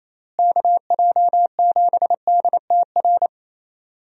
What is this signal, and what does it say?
Morse code, heard as a keyed tone, about 28 wpm, 700 Hz. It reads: KJ7DTR
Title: USA Call Signs - 6 Characters - Repeated 3x - 28wpm